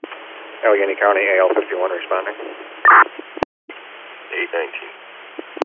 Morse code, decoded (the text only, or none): none